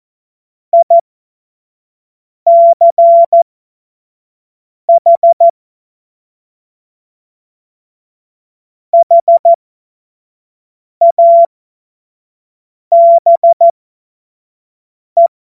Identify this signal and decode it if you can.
Morse code, heard as a keyed tone, about 14 wpm, spaced as for 5 wpm, 680 Hz. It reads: ICH HABE
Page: cw-witze